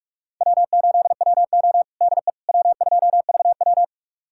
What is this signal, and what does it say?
Morse code, heard as a keyed tone, about 42 wpm, 700 Hz. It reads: W8WO DE W2VW
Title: USA Callsign DE Callsign - 4 Characters - 45wpm